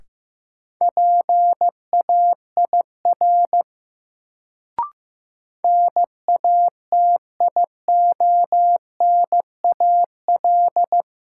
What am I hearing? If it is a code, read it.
Morse code, heard as a keyed tone, about 15 wpm, 700 Hz. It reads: PAIR NATIONAL
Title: Sentences from Top 1200 Words - New Word Drill Alt - 15wpm